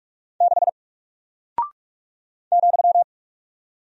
Morse code, heard as a keyed tone, about 45 wpm, 700 Hz. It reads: 6 ,
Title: Single Character (Letter:Number:Punctuation:Prosign) Alt 45wpm